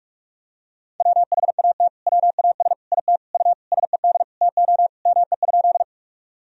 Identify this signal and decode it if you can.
Morse code, heard as a keyed tone, about 46 wpm, 700 Hz. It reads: WHAT WAS IT U SED TO ME?
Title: Sentences from Top 200 Words and Abbreviations 45wpm